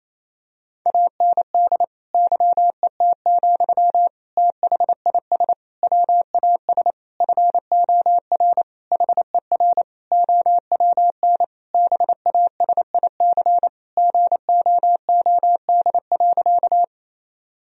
Morse code, heard as a keyed tone, about 28 wpm, 700 Hz. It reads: AND YET, THIS WAS FOR HER OWN BASIC GOOD.
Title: Sentences from Top 1000 Words - Review Part 3 - 28wpm